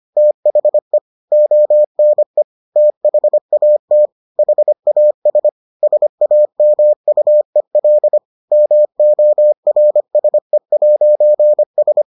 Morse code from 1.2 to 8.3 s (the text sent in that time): ONE THAT HAS SAMUEL